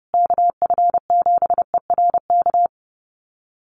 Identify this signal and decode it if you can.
Morse code, heard as a keyed tone, about 30 wpm, 700 Hz. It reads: KF7ERK
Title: USA Call Signs Repeated 3x 30wpm